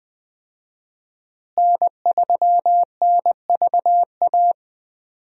Morse code, heard as a keyed tone, about 21 wpm, 700 Hz. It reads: N3NVA
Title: USA Call Signs - 5 Characters - Repeated 3x - 20wpm